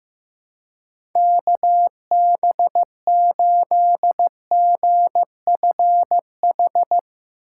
Morse code, heard as a keyed tone, about 15 wpm, 700 Hz. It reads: KB8GFH
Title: Speed Racing - USA 2x3 Call Signs 15wpm